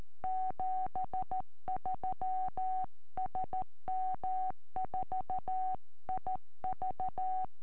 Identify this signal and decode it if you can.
Morse code, heard as a keyed tone, about 13 wpm, 750 Hz. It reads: 73 SM4IV
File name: eme / sm4ive 70cm